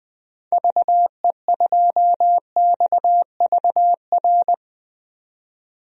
Morse code, heard as a keyed tone, about 20 wpm, 700 Hz. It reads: VE2XVR
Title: DXCC-100 Call Signs - 6 Characters - 20wpm